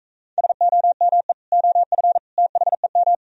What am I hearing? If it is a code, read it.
Morse code, heard as a keyed tone, about 43 wpm, 700 Hz. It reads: SOME OF THEM